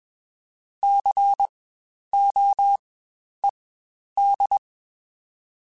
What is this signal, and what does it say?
Morse code, heard as a keyed tone, about 21 wpm, spaced as for 10 wpm, 800 Hz. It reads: COED
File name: Spaced Alphas / Spaced Alphas mp3s / Cs sp @20